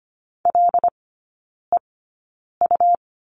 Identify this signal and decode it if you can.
Morse code, heard as a keyed tone, about 25 wpm, spaced as for 9 wpm, 700 Hz. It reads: LEU
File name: Lesson 17e - Letter u - Three Characters 25wpm